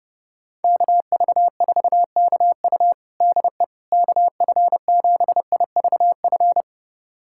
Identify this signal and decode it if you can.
Morse code, heard as a keyed tone, about 30 wpm, 700 Hz. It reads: KV4KU DE KF7IVF
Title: USA Callsign DE Callsign 30wpm